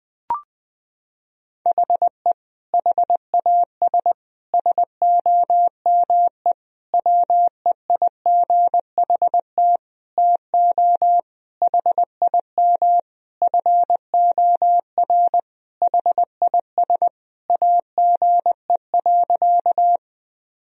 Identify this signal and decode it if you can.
Morse code, heard as a keyed tone, about 20 wpm, 700 Hz. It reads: HE HAS SOME WEIGHT TO HIM FOR HIS AGE.
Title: Sentences from Top 1000 Words - Review Part 3 - 20wpm